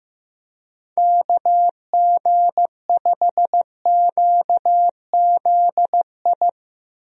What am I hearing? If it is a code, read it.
Morse code, heard as a keyed tone, about 15 wpm, 700 Hz. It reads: KG5QZI
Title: USA Call Signs - 6 Characters - Repeated 3x - 15wpm